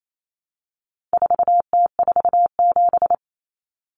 Morse code, heard as a keyed tone, about 28 wpm, 700 Hz. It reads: 4T47
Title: Common Cut Numbers - 4-Digit 28wpm